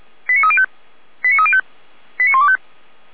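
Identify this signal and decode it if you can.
Five-tone selective call (ZVEI), heard as quick runs of five tones, tones of about 70 ms. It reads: 89305 89305 89125
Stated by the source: ZVEI five-tone call 89097 received by SDR.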